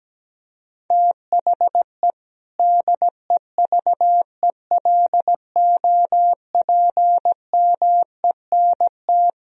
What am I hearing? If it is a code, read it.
Morse code, heard as a keyed tone, about 17 wpm, 700 Hz. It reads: THE DEVELOPMENT
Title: Sentences from Top 1000 Words - Review Part 4 - 17wpm